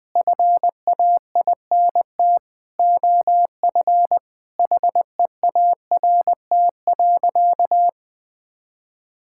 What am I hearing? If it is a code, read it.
Morse code, heard as a keyed tone, about 19 wpm, 700 Hz. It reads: FAINT OF HEART.
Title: Sentences with 6-Word Phrases - Spoken Beginning - 20wpm